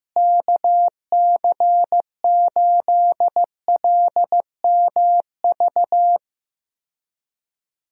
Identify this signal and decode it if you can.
Morse code, heard as a keyed tone, about 15 wpm, 700 Hz. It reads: KC8LMV
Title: USA Call Signs - 6 Characters - Repeated 2x - 15wpm